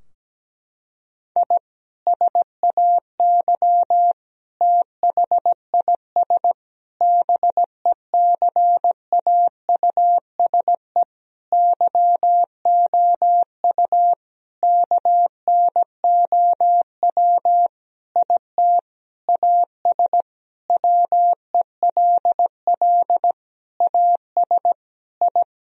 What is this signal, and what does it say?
Morse code, heard as a keyed tone, about 17 wpm, 700 Hz. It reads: I SAY THIS BECAUSE YOU KNOW IT AS WELL AS I
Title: Sentences from Top 1000 Words - Review Part 3 - 17wpm